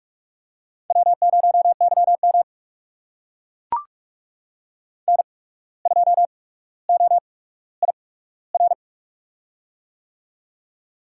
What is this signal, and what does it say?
Morse code, heard as a keyed tone, about 44 wpm, spaced as for 18 wpm, 700 Hz. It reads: W0YM N2OIR
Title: USA Call Signs - 8x Character Spacing - 45wpm